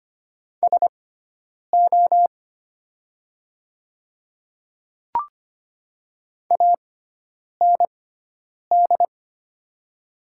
Morse code, heard as a keyed tone, about 25 wpm, spaced as for 9 wpm, 700 Hz. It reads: SO AND
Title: 6x Character Spacing - Top 100 Words - 25wpm